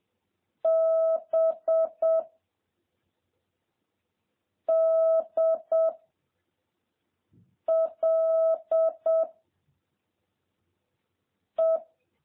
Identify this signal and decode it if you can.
Morse code, heard as a keyed tone, about 7 wpm, 660 Hz. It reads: B D L E